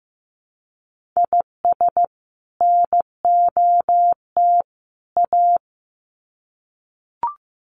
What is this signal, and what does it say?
Morse code, heard as a keyed tone, about 15 wpm, 700 Hz. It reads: IS NOT A
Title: Sets of 3 Words 15wpm